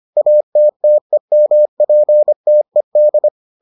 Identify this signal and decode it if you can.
Morse code, heard as a keyed tone, about 25 wpm, 600 Hz. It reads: ATTEMPTED